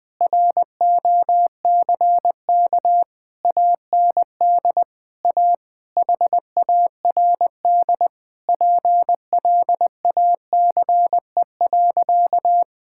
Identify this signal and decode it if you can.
Morse code, heard as a keyed tone, about 20 wpm, 700 Hz. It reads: ROCK AND A HARD PLACE.
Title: Sentences with 8-Word Phrases - 20wpm